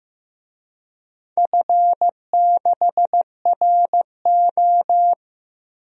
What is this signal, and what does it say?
Morse code, heard as a keyed tone, about 15 wpm, 700 Hz. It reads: F6RO
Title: DXCC-100 Call Signs - 4 Characters - 15wpm